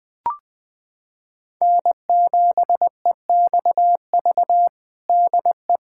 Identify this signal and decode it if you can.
Morse code, heard as a keyed tone, about 20 wpm, 700 Hz. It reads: N7EXV DE
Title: USA Callsign DE Callsign - 5 Characters - 20wpm